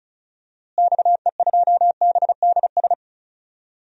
Morse code, heard as a keyed tone, about 34 wpm, 700 Hz. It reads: XE2BDS
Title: DXCC-100 Call Signs - 6 Characters - Repeated 3x - 35wpm